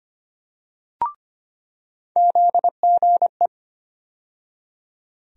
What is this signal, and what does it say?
Morse code, heard as a keyed tone, about 25 wpm, 700 Hz. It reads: ZGE
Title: USA Call Sign Postfixes - 3 Letters 25wpm